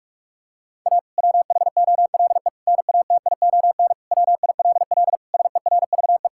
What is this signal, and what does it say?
Morse code, heard as a keyed tone, about 46 wpm, 700 Hz. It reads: A WHOLE NATION WILL SERVE
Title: Sentences from Top 1000 Words - Review Part 6 - 45wpm